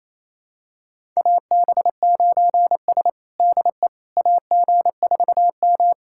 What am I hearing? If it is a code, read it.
Morse code, heard as a keyed tone, about 28 wpm, 700 Hz. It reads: AB9S DE AG4M
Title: USA Callsign DE Callsign - 4 Characters - 28wpm